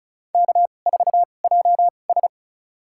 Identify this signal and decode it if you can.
Morse code, heard as a keyed tone, about 36 wpm, spaced as for 25 wpm, 690 Hz. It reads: K4JS
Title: USA Call Signs - 4 Characters - 2x Character Spacing - 35wpm